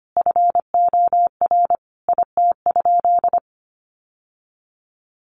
Morse code, heard as a keyed tone, about 25 wpm, 700 Hz. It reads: FOR IT?
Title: Sentences from Top 900 Words 25wpm